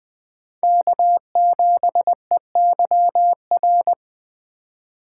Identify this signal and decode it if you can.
Morse code, heard as a keyed tone, about 24 wpm, 700 Hz. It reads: K7EYR
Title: Speed Racing - USA Call Signs 20wpm